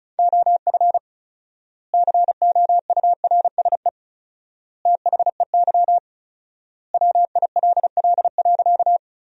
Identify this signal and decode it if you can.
Morse code, heard as a keyed tone, about 34 wpm, 700 Hz. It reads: OF COURSE THEY WILL.
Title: 4x Word-Spacing - Sentences from Top 500 Words - 35wpm